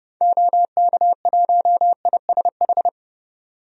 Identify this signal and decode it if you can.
Morse code, heard as a keyed tone, about 32 wpm, 700 Hz. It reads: OK1ISH
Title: DXCC-100 Call Signs - 6 Characters - 30wpm